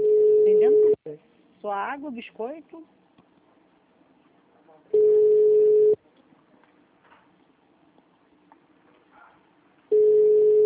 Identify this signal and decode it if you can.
Ringback tone (Europe, 425 Hz), with speech.